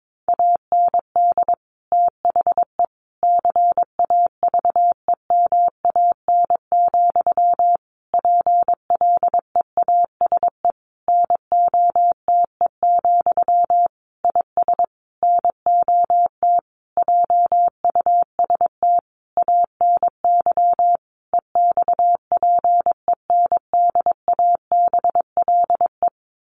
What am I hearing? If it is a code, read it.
Morse code, heard as a keyed tone, about 22 wpm, 700 Hz. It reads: AND THE CAVEMAN, PLEASE NOTE, IS NOT JUST ANY EXPENDABLE
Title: A Sound Of Thunder 22wpm